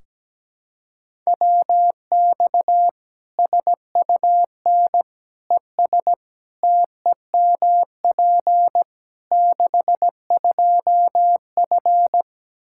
Morse code, heard as a keyed tone, about 17 wpm, 700 Hz. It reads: WX SUN ES TEMP 62F
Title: Practice QSO - Exchange 2 - Info - 17wpm